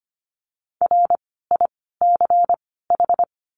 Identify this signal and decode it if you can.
Morse code, heard as a keyed tone, about 25 wpm, 700 Hz. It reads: R I C H
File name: course / 25:10wpm / 094 Lesson 16f - Letter c - Words 25wpm